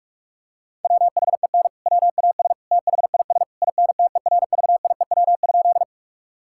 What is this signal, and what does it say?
Morse code, heard as a keyed tone, about 46 wpm, 700 Hz. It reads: WHEN WAS THIS INTERVIEW?
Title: Sentences from Top 1600 Words 45wpm